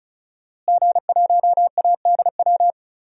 Morse code, heard as a keyed tone, about 36 wpm, 700 Hz. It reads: G1ADW
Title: DXCC-100 Call Signs - 5 Characters - Repeated 2x - 35wpm